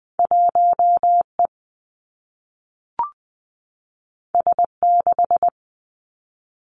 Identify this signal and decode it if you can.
Morse code, heard as a keyed tone, about 19 wpm, 700 Hz. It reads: 1 E S 6
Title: Two Challenging Letter-Number Combinations 20wpm